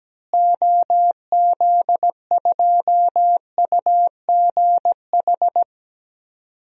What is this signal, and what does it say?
Morse code, heard as a keyed tone, about 17 wpm, 700 Hz. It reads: OZ2UGH
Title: DXCC-100 Call Signs - 6 Characters - Repeated 3x - 17wpm